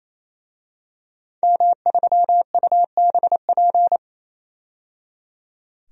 Morse code, heard as a keyed tone, about 28 wpm, 700 Hz. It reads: M3UBP
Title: DXCC-100 Call Signs - 5 Characters - 28wpm